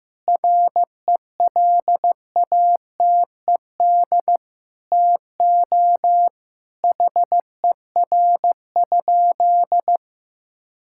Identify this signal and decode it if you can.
Morse code, heard as a keyed tone, about 16 wpm, 690 Hz. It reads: RELATED TO HER?
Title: Sentences from Top 1000 Words - Review Part 5 - 15wpm